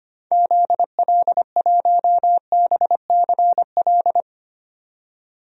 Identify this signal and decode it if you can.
Morse code, heard as a keyed tone, about 25 wpm, 700 Hz. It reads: ZL1BCL
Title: DXCC-100 Call Signs - 6 Characters - 25wpm